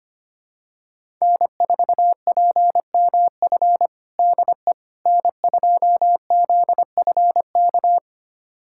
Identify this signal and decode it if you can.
Morse code, heard as a keyed tone, about 25 wpm, 700 Hz. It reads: N4PMF DE N2ZFK
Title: USA Callsign DE Callsign - 5 Characters - 25wpm